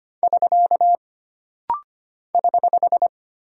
Morse code, heard as a keyed tone, about 25 wpm, 700 Hz. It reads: <SK> <HH>
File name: Single Punctuation-Prosign - Rapid-Fire - 25wpm